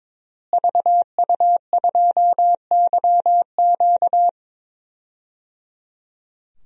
Morse code, heard as a keyed tone, about 22 wpm, 700 Hz. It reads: VU2YQ
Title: DXCC-100 Call Signs - 5 Characters - Repeated 3x - 22wpm